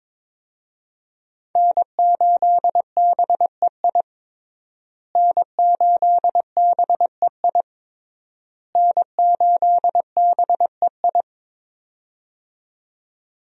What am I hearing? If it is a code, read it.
Morse code, heard as a keyed tone, about 21 wpm, 700 Hz. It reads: N8BEI N8BEI N8BEI
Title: USA Call Signs - 5 Characters - Repeated 3x - 22wpm